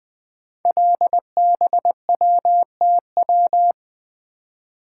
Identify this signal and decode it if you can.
Morse code, heard as a keyed tone, about 20 wpm, 700 Hz. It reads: LBWTW
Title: Code Groups - 5 Letter:Numbers - 20wpm